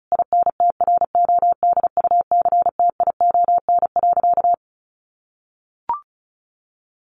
Morse code, heard as a keyed tone, about 34 wpm, 700 Hz. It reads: INTRODUCTION.
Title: Sentences from Top 1000 Words 35wpm